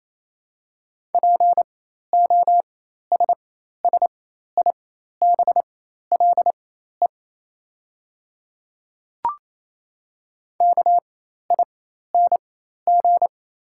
Morse code, heard as a keyed tone, about 28 wpm, spaced as for 13 wpm, 700 Hz. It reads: POSSIBLE KING
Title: 4x Character Spacing - Top 500 Words - 28wpm